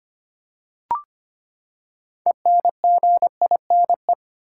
Morse code, heard as a keyed tone, about 25 wpm, 700 Hz. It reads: ENGINE
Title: Six Letter Words - 25wpm